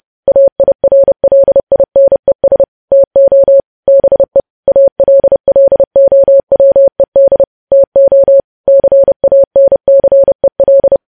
Morse code, heard as a keyed tone, about 30 wpm, 550 Hz. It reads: AIRLINES TO BE ALLOWED TO CANCEL